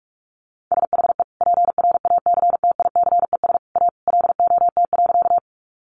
Morse code, heard as a keyed tone, about 44 wpm, 700 Hz. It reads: SHE PRACTICES A LOT.
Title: Additional Easy Phrases 45wpm